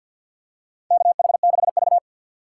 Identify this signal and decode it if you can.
Morse code, heard as a keyed tone, about 53 wpm, 700 Hz. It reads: KH6V
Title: USA Call Signs - 4 Characters - 6x Character Spacing - 50wpm